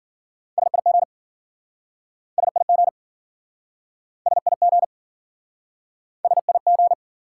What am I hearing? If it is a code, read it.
Morse code, heard as a keyed tone, about 52 wpm, 700 Hz. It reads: SIG SIG SIG SIG